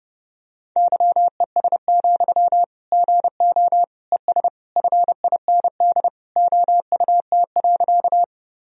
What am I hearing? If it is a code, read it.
Morse code, heard as a keyed tone, about 32 wpm, 700 Hz. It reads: YES, GO ES FIND OUT.
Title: Sentences from Top 500 Words and Abbreviations 30wpm